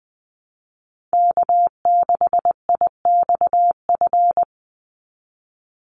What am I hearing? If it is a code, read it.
Morse code, heard as a keyed tone, about 20 wpm, 700 Hz. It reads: K6IXF
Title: USA Call Signs 20wpm